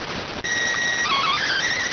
SSTV (slow-scan television), PD 290.